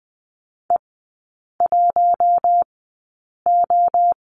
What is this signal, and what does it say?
Morse code, heard as a keyed tone, about 20 wpm, spaced as for 8 wpm, 700 Hz. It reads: E1O
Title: 041 Lesson 8e - Number 1 - Three Characters 20wpm